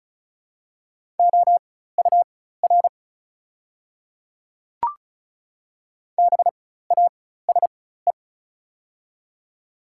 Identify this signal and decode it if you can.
Morse code, heard as a keyed tone, about 34 wpm, spaced as for 16 wpm, 690 Hz. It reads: OUR BASE
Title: 4x Character Spacing - Top 500 Words - 35wpm